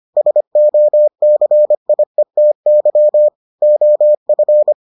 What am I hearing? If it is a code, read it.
Morse code, heard as a keyed tone, about 25 wpm, 600 Hz. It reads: SOCIETY OF